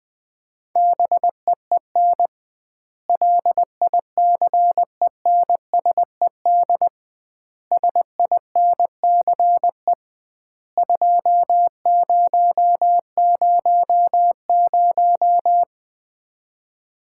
Morse code, heard as a keyed tone, about 20 wpm, 700 Hz. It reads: BEEN LICENSED SINCE 2000
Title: Common QSO Phrases - 2x Word-Spacing - 20wpm